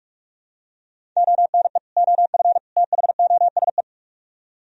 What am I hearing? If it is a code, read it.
Morse code, heard as a keyed tone, about 44 wpm, 700 Hz. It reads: ONE OF THOSE